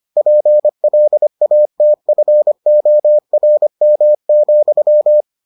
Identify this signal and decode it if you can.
Morse code, heard as a keyed tone, about 25 wpm, 600 Hz. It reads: PLATFORM,